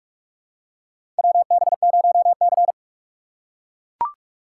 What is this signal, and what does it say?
Morse code, heard as a keyed tone, about 42 wpm, 700 Hz. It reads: WB0C
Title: USA Call Signs - 2x Character Spacing - 45wpm